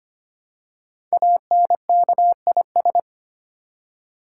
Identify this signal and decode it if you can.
Morse code, heard as a keyed tone, about 25 wpm, 700 Hz. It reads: ANKIS
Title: Code Groups - 5 Letters - 25wpm